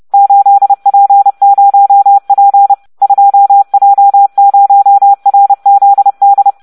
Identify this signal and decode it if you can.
Morse code, heard as a keyed tone, about 30 wpm, 800 Hz. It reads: 8P0P 2J0RZD